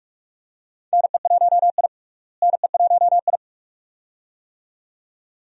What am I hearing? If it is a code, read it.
Morse code, heard as a keyed tone, about 47 wpm, 700 Hz. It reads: NE1I NE1I
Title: USA Call Signs - 4 Characters - Repeated 2x - 45wpm